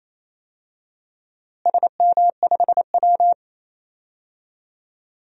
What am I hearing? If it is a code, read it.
Morse code, heard as a keyed tone, about 27 wpm, 700 Hz. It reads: SM5W
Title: DXCC-100 Call Signs - 4 Characters - Repeated 3x - 28wpm